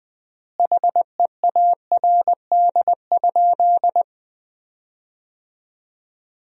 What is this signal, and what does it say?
Morse code, heard as a keyed tone, about 20 wpm, 700 Hz. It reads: HEARD?
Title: 3x Word-Spacing - Sentences from Top 400 Words - 20wpm